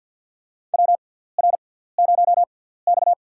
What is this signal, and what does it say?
Morse code, heard as a keyed tone, about 50 wpm, spaced as for 17 wpm, 690 Hz. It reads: WR0X
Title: USA Call Signs - 4 Characters - 6x Character Spacing - 50wpm